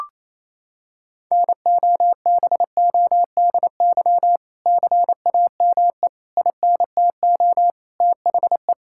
Morse code, heard as a keyed tone, about 28 wpm, 700 Hz. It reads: NOBODY CAME INTO THE
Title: Sentences from Top 1600 Words 28wpm